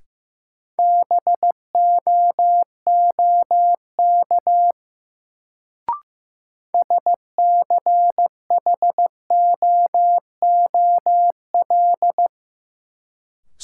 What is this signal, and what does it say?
Morse code, heard as a keyed tone, about 15 wpm, 700 Hz. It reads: BOOK SCHOOL
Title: Sentences from Top 300 Words - New Word Drill - 15wpm